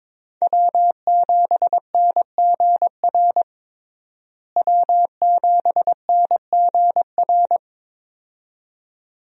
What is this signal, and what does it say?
Morse code, heard as a keyed tone, about 21 wpm, 700 Hz. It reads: W7NGR W7NGR
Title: USA Call Signs - 5 Characters - Repeated 2x - 22wpm